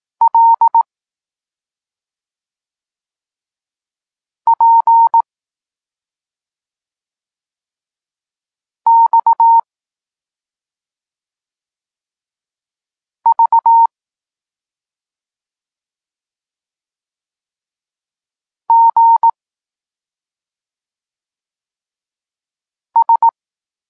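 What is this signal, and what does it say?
Morse code, heard as a keyed tone, about 18 wpm, 920 Hz. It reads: L P X V G S
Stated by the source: Morse code lesson for learning the letters K and X, from the SP9MOA Niepołomicki Klub Krótkofalowców course.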